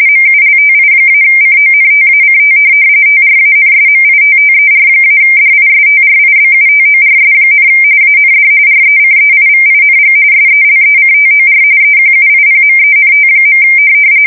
RTTY, about 45.45 baud. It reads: MULTI-CONTACT TRANSMITTER-DISTRIBUTORS, AND RECEIVING SELECTOR EQUIPMENT.
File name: fsk170 / teletype-corp